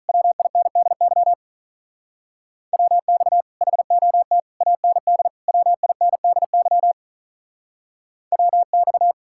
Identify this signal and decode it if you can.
Morse code, heard as a keyed tone, about 45 wpm, 700 Hz. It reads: WINDY WX HOT AND WINDY WX